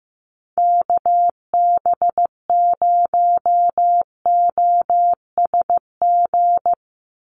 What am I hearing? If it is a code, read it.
Morse code, heard as a keyed tone, about 15 wpm, 700 Hz. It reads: KB0OSG
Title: USA Call Signs 15wpm